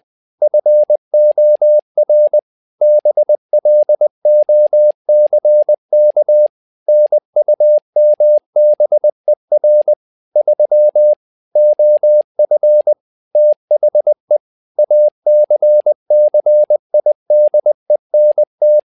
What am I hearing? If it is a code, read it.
Morse code, heard as a keyed tone, about 20 wpm, 600 Hz. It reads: FOR BLOCK NUMBER 3 OF THE ACCIDENT
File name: tower trouble 20wpm